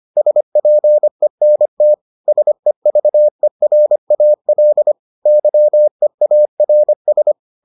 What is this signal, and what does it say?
Morse code, heard as a keyed tone, about 25 wpm, 600 Hz. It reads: SPENT SEVERAL YEARS